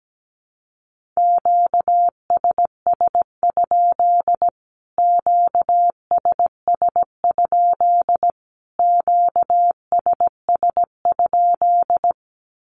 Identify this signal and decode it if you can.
Morse code, heard as a keyed tone, about 17 wpm, 700 Hz. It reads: QSS? QSS? QSS?
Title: Q-Codes Repeated 3x 17wpm